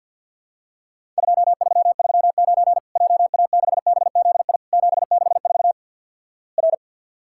Morse code, heard as a keyed tone, about 50 wpm, 700 Hz. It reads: 2339 JA6BZI 764 R
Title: Worked All Europe DX Contest - QTC Exchange with Repeats - 50wpm